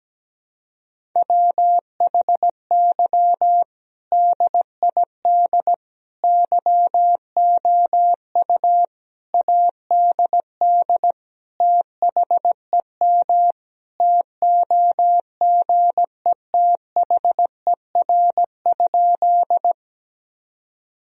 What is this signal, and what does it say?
Morse code, heard as a keyed tone, about 17 wpm, 700 Hz. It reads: WHY DID YOU ADD THEM TOGETHER?
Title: Sentences from Top 1000 Words - Review Part 1 - 17wpm